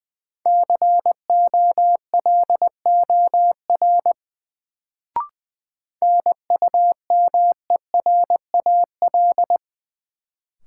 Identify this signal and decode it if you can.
Morse code, heard as a keyed tone, about 20 wpm, 700 Hz. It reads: COLOR NUMERAL
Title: Sentences from Top 1000 Words - Review Part 2 - New Word Drill - 20wpm